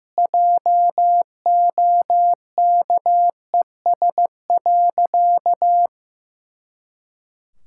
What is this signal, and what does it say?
Morse code, heard as a keyed tone, about 15 wpm, 700 Hz. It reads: JOKES.
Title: Pangram Challenge v2 - 15wpm